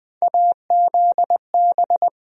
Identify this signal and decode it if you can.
Morse code, heard as a keyed tone, about 20 wpm, 700 Hz. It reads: AZB
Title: USA Call Sign Postfixes - 3 Letters 20wpm